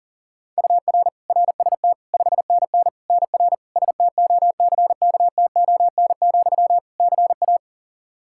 Morse code, heard as a keyed tone, about 40 wpm, 700 Hz. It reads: UR RST 5NN NR STOCKTON, CA
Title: Practice QSO - Exchange 1 - Beginning - 40wpm